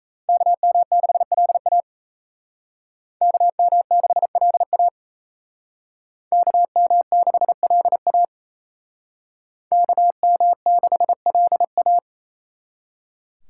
Morse code, heard as a keyed tone, about 35 wpm, 700 Hz. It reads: KM6LA KM6LA KM6LA KM6LA